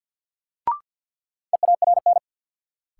Morse code, heard as a keyed tone, about 48 wpm, 700 Hz. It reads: EARN
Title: Sentences from Top 1500 Words - New Word Drill Alt - 50wpm